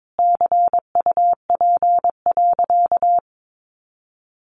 Morse code, heard as a keyed tone, about 22 wpm, 700 Hz. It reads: CUP.
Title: Easy Phrases 22wpm